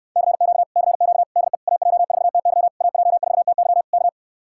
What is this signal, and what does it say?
Morse code, heard as a keyed tone, about 68 wpm, 700 Hz. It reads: CQ CQ DE AJ4TY AJ4TY K